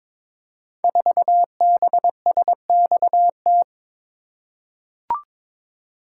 Morse code, heard as a keyed tone, about 22 wpm, 700 Hz. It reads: 4BSXT
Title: Code Groups - 5 Letter:Numbers - 22wpm